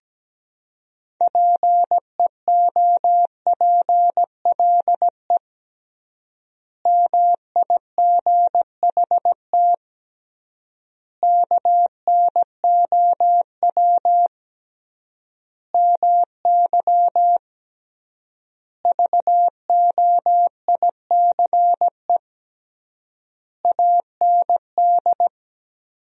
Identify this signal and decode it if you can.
Morse code, heard as a keyed tone, about 17 wpm, 700 Hz. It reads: PEOPLE MIGHT KNOW MY VOICE AND
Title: 3x Word-Spacing - Sentences from Top 500 Words - 17wpm